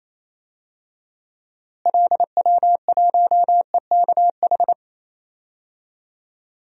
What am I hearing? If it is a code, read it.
Morse code, heard as a keyed tone, about 28 wpm, 700 Hz. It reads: LW1EKH
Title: DXCC-100 Call Signs - 6 Characters - Repeated 2x - 28wpm